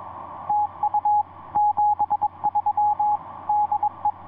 Morse code, heard as a keyed tone, about 21 wpm, 840 Hz, with radio noise. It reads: TU 73 DE
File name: ce1-k7ca ea8tl etc 17m cw 10jan2010 1620